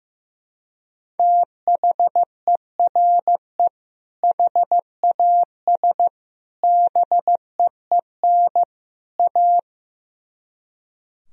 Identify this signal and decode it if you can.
Morse code, heard as a keyed tone, about 15 wpm, 700 Hz. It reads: THERE HAS BEEN A
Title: Speed-Racing - Sets of 4 Words 15wpm